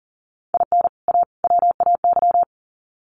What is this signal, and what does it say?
Morse code, heard as a keyed tone, about 40 wpm, 700 Hz. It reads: IN A WAY
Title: Sets of 3 Words 40wpm